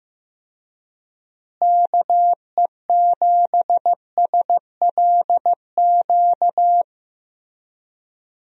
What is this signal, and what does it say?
Morse code, heard as a keyed tone, about 15 wpm, 700 Hz. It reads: KE7SLQ
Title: USA Call Signs - 6 Characters - Repeated 3x - 15wpm